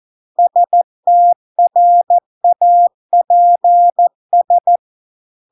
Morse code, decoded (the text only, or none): STRAPS